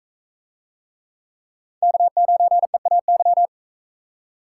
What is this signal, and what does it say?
Morse code, heard as a keyed tone, about 42 wpm, 700 Hz. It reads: K9EAY